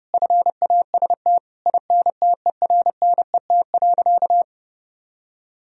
Morse code, heard as a keyed tone, about 30 wpm, 700 Hz. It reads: FAST INTERNET.
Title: Sentences from Top 1000 Words - Review Part 4 - 30wpm